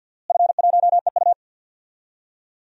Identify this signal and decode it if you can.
Morse code, heard as a keyed tone, about 46 wpm, 700 Hz. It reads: F1EU
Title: DXCC-100 Call Signs - 4 Characters - Repeated 2x - 50wpm